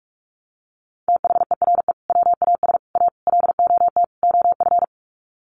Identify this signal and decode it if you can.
Morse code, heard as a keyed tone, about 44 wpm, 700 Hz. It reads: THERE WAS A LOT OF
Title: Sets of 5 Words 45wpm